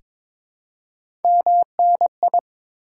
Morse code, heard as a keyed tone, about 22 wpm, 690 Hz. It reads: MNI
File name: Top 90 QSO Elements - 4x Character Spacing - 22wpm